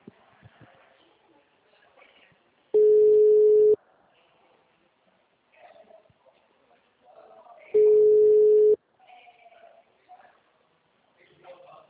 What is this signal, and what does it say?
Ringback tone (Europe, 425 Hz), with speech.
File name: out- -2015